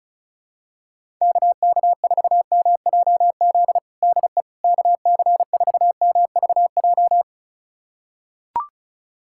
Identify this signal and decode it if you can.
Morse code, heard as a keyed tone, about 34 wpm, 700 Hz. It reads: KK4MJZ DE KC4MVJ
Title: USA Callsign DE Callsign - 6 Characters - 35wpm